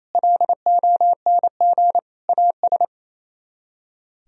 Morse code, heard as a keyed tone, about 34 wpm, 700 Hz. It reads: LONG AS
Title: Speed-Racing - Sets of 3 Words 28wpm